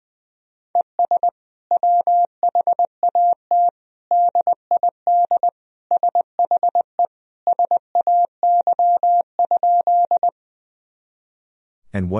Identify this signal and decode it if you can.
Morse code, heard as a keyed tone, about 21 wpm, 700 Hz. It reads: ES WHAT DID SHE SAY?
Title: Sentences from Top 200 Words and Abbreviations 20wpm